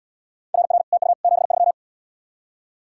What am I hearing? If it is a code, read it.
Morse code, heard as a keyed tone, about 74 wpm, 700 Hz. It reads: RR TU 73